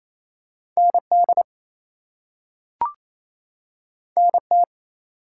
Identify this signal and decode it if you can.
Morse code, heard as a keyed tone, about 28 wpm, 700 Hz. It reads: ND NT
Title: US States and Canadian Provinces 28wpm